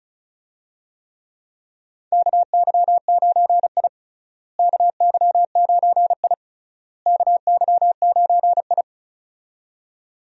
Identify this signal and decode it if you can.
Morse code, heard as a keyed tone, about 36 wpm, 700 Hz. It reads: KY9I KY9I KY9I
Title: USA Call Signs - 4 Characters - Repeated 3x - 35wpm